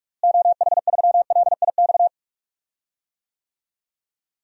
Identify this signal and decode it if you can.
Morse code, heard as a keyed tone, about 46 wpm, 700 Hz. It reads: OH3LIX
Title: DXCC-100 Call Signs - 6 Characters - Repeated 2x - 45wpm